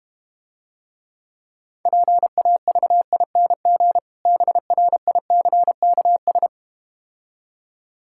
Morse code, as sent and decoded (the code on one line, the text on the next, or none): .--. .- ...- .. -. --. / -... .-. .. -.-. -.- ...
PAVING BRICKS